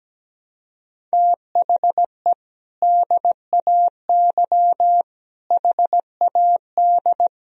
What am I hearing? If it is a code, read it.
Morse code, heard as a keyed tone, about 17 wpm, 700 Hz. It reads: THE DAY HAD
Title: Sentences from Top 1000 Words - Review Part 2 - 17wpm